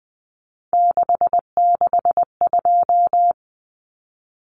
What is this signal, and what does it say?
Morse code, heard as a keyed tone, about 20 wpm, 700 Hz. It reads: 662
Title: Common Cut Numbers - 3-Digit 20wpm